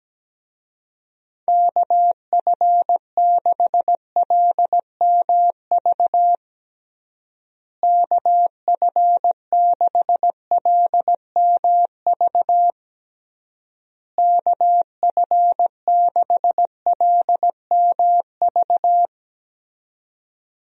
Morse code, heard as a keyed tone, about 17 wpm, 700 Hz. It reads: KF6LMV KF6LMV KF6LMV
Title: USA Call Signs - 6 Characters - Repeated 3x - 17wpm